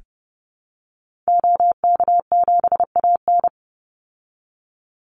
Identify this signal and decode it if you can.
Morse code, heard as a keyed tone, about 30 wpm, 700 Hz. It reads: OK7AN
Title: Call Signs Repeated 3x - 30wpm